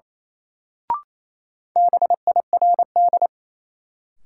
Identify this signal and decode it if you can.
Morse code, heard as a keyed tone, about 28 wpm, 700 Hz. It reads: BIRD
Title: Sentences from Top 400 Words - New Word Drill - 28wpm